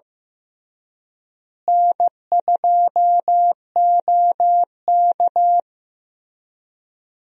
Morse code, heard as a keyed tone, about 15 wpm, 700 Hz. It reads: N2OK
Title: USA Call Signs - 4 Characters - Repeated 3x - 15wpm